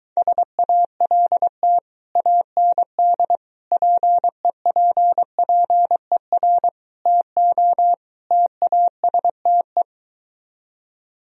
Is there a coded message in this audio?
Yes: Morse code from 0.2 to 9.8 s, heard as a keyed tone, about 22 wpm, 700 Hz. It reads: SALT AND PEPPER TO TASTE